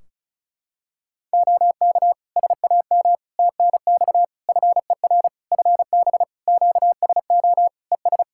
Morse code, heard as a keyed tone, about 36 wpm, 700 Hz. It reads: OK SAM TNX FER FB QSO ES
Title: Practice QSO - Exchange 4 - Ending - 35wpm